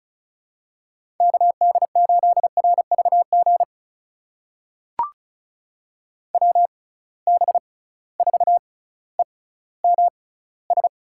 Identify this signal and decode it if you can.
Morse code, heard as a keyed tone, about 36 wpm, spaced as for 18 wpm, 700 Hz. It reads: KD8RVG WB4EMS
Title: USA Call Signs - 6 Characters - 6x Character Spacing - 35wpm